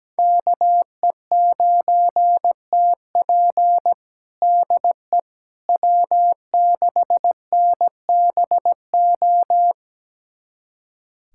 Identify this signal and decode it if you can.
Morse code, heard as a keyed tone, about 17 wpm, 700 Hz. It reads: KE9TP DE W6NBO
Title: USA Callsign DE Callsign - 5 Characters - 17wpm